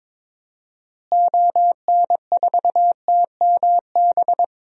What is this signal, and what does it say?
Morse code, heard as a keyed tone, about 22 wpm, 700 Hz. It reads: ON4TMB
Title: DXCC-100 Call Signs - 6 Characters - Repeated 2x - 22wpm